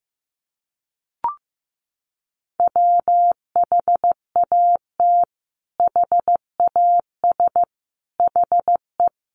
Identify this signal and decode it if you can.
Morse code, heard as a keyed tone, about 15 wpm, 700 Hz. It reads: WHAT HAS HE
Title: Sentences from Top 500 Words 15wpm